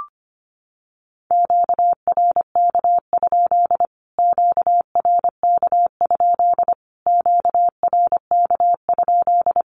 Morse code, heard as a keyed tone, about 25 wpm, 700 Hz. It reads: QRK? QRK? QRK?
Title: Q-Codes Repeated 3x 25wpm